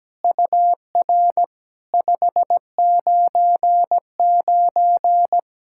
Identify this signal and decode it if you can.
Morse code, heard as a keyed tone, about 17 wpm, 660 Hz. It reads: UR 599
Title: SOTA QSO Practice 17wpm